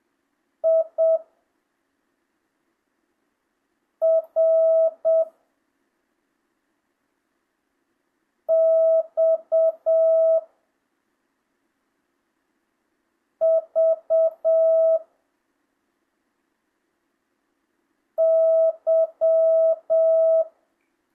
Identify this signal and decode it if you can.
Morse code, heard as a keyed tone, about 7 wpm, 650 Hz. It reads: I R X V Y
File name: jh8cbh / 2025cw